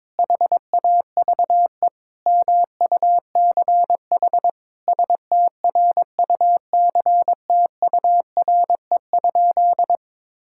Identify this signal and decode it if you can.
Morse code, heard as a keyed tone, about 22 wpm, 700 Hz. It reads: HAVE MUCH STRUCTURE?
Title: Sentences from Top 1000 Words - Review Part 6 - 22wpm